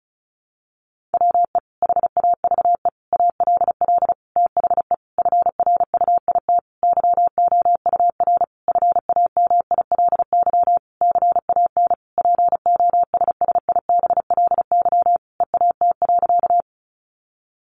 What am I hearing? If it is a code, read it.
Morse code, heard as a keyed tone, about 34 wpm, 700 Hz. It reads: WE HAVE ALL THE FRUIT YOUR FAMILY CAN POSSIBLY EAT.
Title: Sentences from Top 1300 Words 35wpm